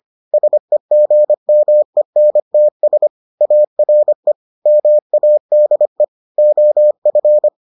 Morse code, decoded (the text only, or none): SEGMENTS ARE MADE OF